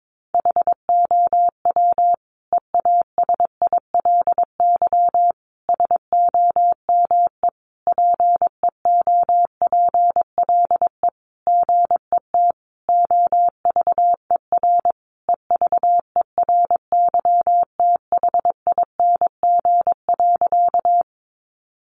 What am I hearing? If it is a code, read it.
Morse code, heard as a keyed tone, about 22 wpm, 700 Hz. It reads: HOW EASILY SOME PEOPLE GET OVER EVERYTHING.
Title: Sentences from Top 900 Words 22wpm